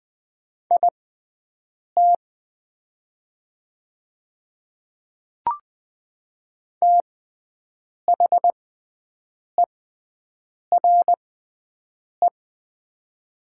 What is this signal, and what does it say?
Morse code, heard as a keyed tone, about 20 wpm, spaced as for 7 wpm, 700 Hz. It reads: IT THERE
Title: 6x Character Spacing - Top 100 Words - 20wpm